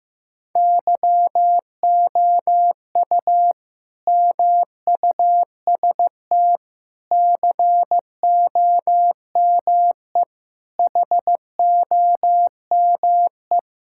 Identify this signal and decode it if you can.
Morse code, heard as a keyed tone, about 15 wpm, 700 Hz. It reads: YOU MUST COME HOME
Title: Sentences from Top 1000 Words - Review Part 5 - 15wpm